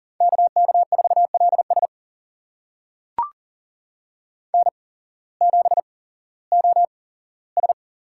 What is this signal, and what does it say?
Morse code, heard as a keyed tone, about 38 wpm, spaced as for 18 wpm, 700 Hz. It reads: KK4LS N7OS
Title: USA Call Signs - 5 Characters - 8x Character Spacing - 40wpm